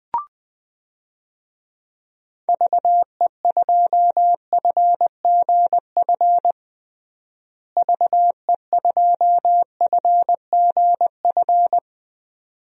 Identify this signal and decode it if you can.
Morse code, heard as a keyed tone, about 20 wpm, 700 Hz. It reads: VE2FGF VE2FGF
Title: DXCC-100 Call Signs - 6 Characters - Repeated 2x - 20wpm